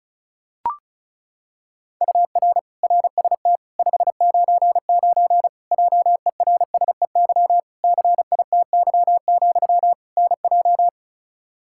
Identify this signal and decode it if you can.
Morse code, heard as a keyed tone, about 36 wpm, 700 Hz. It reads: UR RST 599 JERSEY CITY, NJ
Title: Practice QSO - Exchange 1 - Beginning - 35wpm